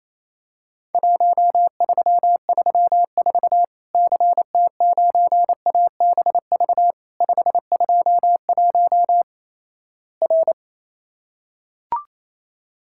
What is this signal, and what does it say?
Morse code, heard as a keyed tone, about 28 wpm, 700 Hz. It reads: 1334 CT9ABV 521 R
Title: Worked All Europe DX Contest - QTC Exchange - 28wpm